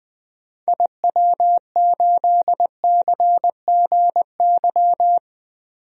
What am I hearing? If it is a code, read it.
Morse code, heard as a keyed tone, about 20 wpm, 700 Hz. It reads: IW8CGY
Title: DXCC-100 Call Signs - 6 Characters - Repeated 3x - 20wpm